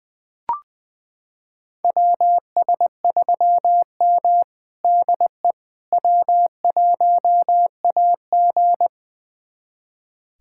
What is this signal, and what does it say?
Morse code, heard as a keyed tone, about 21 wpm, 700 Hz. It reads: WS3M DE W1AG
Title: USA Callsign DE Callsign - 4 Characters - 20wpm